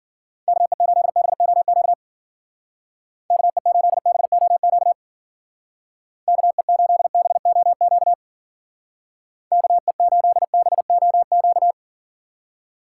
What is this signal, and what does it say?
Morse code, heard as a keyed tone, about 52 wpm, 700 Hz. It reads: KE8BOQ KE8BOQ KE8BOQ KE8BOQ